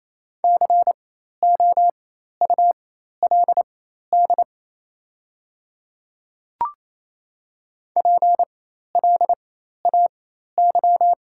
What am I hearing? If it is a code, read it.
Morse code, heard as a keyed tone, about 28 wpm, spaced as for 13 wpm, 690 Hz. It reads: COULD PLAY
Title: 4x Character Spacing - Top 200 Words - 28wpm